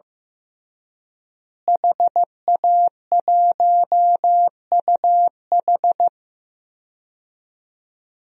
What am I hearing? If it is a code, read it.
Morse code, heard as a keyed tone, about 15 wpm, 700 Hz. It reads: HA1UH
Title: DXCC-100 Call Signs - 5 Characters - Repeated 3x - 15wpm